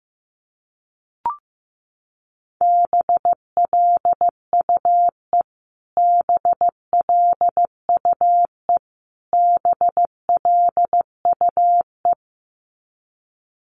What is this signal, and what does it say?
Morse code, heard as a keyed tone, about 15 wpm, 700 Hz. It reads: BLUE BLUE BLUE
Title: Top 500 Words Repeated 15wpm